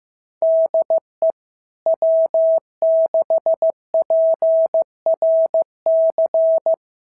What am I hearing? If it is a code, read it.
Morse code, heard as a keyed tone, about 15 wpm, 650 Hz. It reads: DE W6PRC
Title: SOTA QSO Practice 15wpm